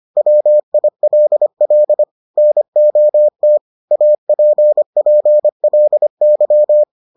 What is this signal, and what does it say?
Morse code, heard as a keyed tone, about 25 wpm, 600 Hz. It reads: WILL NOT APPLY